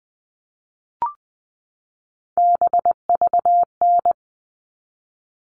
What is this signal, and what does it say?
Morse code, heard as a keyed tone, about 20 wpm, 700 Hz. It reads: BVN
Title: All Cut Numbers - 3-Digit 20wpm